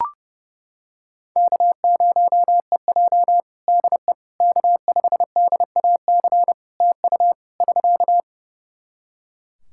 Morse code, heard as a keyed tone, about 30 wpm, 700 Hz. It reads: K0EJ DE K5DAC TU <SK>
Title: Practice QSO - Exchange 4 - Ending - 30wpm